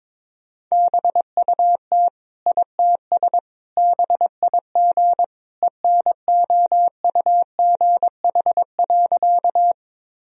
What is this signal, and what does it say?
Morse code, heard as a keyed tone, about 22 wpm, 700 Hz. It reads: BUT ITS BIG ENOUGH.
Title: Sentences from Top 1000 Words - Review Part 1 - 22wpm